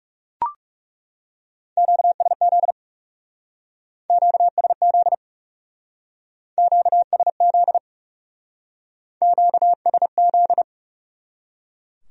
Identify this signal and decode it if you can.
Morse code, heard as a keyed tone, about 37 wpm, 700 Hz. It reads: QSZ QSZ QSZ QSZ
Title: Speed-Racing Q-Codes - 30wpm